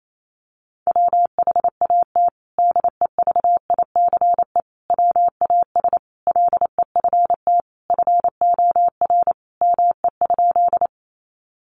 Morse code, heard as a keyed tone, about 28 wpm, 700 Hz. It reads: WHAT DEVICE WAS LEFT FOR ME?
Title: Sentences from Top 1500 Words 28wpm